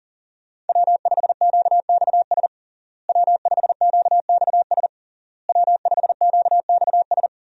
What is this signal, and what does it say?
Morse code, heard as a keyed tone, about 40 wpm, 700 Hz. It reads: W5QXS W5QXS W5QXS
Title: USA Call Signs - 5 Characters - Repeated 3x - 40wpm